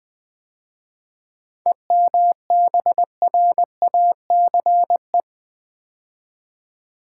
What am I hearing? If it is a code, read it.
Morse code, heard as a keyed tone, about 21 wpm, 700 Hz. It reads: EMBRACE
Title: 7 Letter Words - First 3 Letters Given - 20wpm